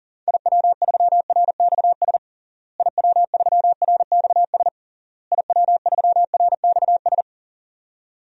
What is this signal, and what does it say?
Morse code, heard as a keyed tone, about 37 wpm, 700 Hz. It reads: IW3RXS IW3RXS IW3RXS
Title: DXCC-100 Call Signs - 6 Characters - Repeated 3x - 40wpm